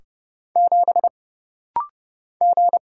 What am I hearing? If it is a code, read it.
Morse code, heard as a keyed tone, about 31 wpm, 700 Hz. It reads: 7 G
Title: Single Letter-Number - Rapid-Fire 30wpm